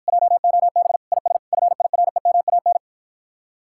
Morse code, heard as a keyed tone, about 54 wpm, 700 Hz. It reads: JOB IS FIREMAN